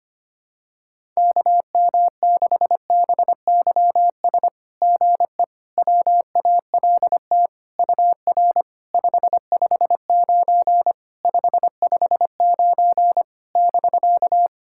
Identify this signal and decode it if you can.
Morse code, heard as a keyed tone, about 25 wpm, 700 Hz. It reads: KM6BYS GE WALT UR 559 559 <BK>
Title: SOTA QSO Practice 25wpm